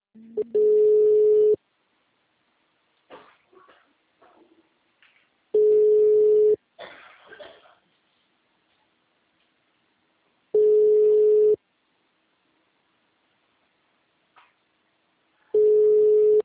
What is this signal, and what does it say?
Ringback tone (Europe, 425 Hz), with speech.